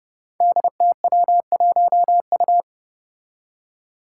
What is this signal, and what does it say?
Morse code, heard as a keyed tone, about 30 wpm, 700 Hz. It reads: DTW1U
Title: Five Challenging Letter-Number Combinations 30wpm